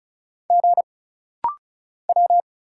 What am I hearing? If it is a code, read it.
Morse code, heard as a keyed tone, about 34 wpm, 700 Hz. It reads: G W
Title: Single Letters - Rapid-Fire 35wpm